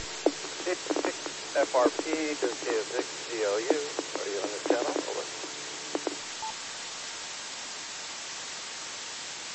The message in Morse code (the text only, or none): none